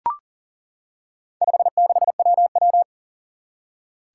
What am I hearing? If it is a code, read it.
Morse code, heard as a keyed tone, about 40 wpm, 700 Hz. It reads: 56WW
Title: Four Challenging Letter-Number Combinations 40wpm